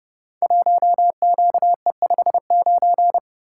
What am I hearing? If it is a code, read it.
Morse code, heard as a keyed tone, about 30 wpm, 700 Hz. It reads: 1QE59
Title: Code Groups - 5 Letter:Numbers - 30wpm